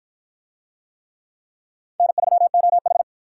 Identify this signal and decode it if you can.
Morse code, heard as a keyed tone, about 52 wpm, 700 Hz. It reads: N3OH